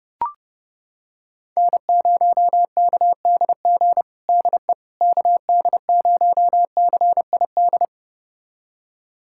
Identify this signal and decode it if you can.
Morse code, heard as a keyed tone, about 32 wpm, 700 Hz. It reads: N0KDG DE KD0CID
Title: USA Callsign DE Callsign 30wpm